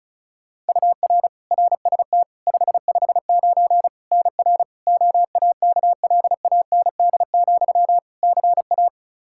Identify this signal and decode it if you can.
Morse code, heard as a keyed tone, about 34 wpm, 700 Hz. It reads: UR RST 559 NR OAKLAND, CA
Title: K1USN SST QSO Practice 35wpm